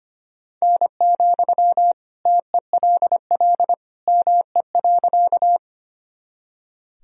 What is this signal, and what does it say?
Morse code, heard as a keyed tone, about 25 wpm, 700 Hz. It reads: N, TELL ME.
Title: Sentences from Top 200 Words and Abbreviations 25wpm